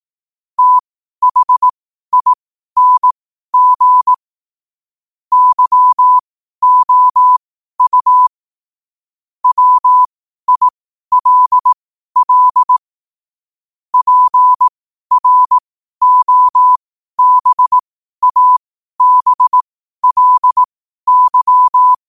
Morse code, heard as a keyed tone, about 18 wpm, spaced as for 12 wpm, 1000 Hz. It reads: THING YOU WILL PROBABLY
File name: qotdfiles / 12wpm